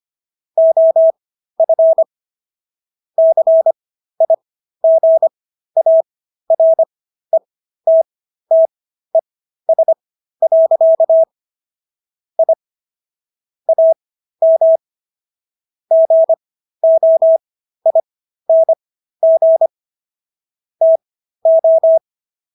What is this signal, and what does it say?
Morse code, heard as a keyed tone, about 25 wpm, spaced as for 13 wpm, 650 Hz. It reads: OF CIGARETTES. I AM GOING TO
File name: SS106 13